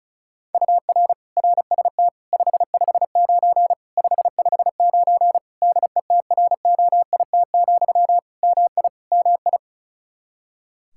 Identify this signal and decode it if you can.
Morse code, heard as a keyed tone, about 34 wpm, 700 Hz. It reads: UR RST 559 559 DETROIT, MI MI
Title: Practice QSO - Exchange 1 - Beginning - 35wpm